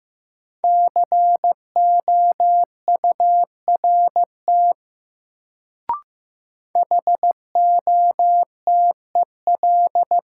Morse code, heard as a keyed tone, about 15 wpm, 700 Hz. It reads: COURT HOTEL
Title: Sentences from Top 1600 Words - New Word Drill - 15wpm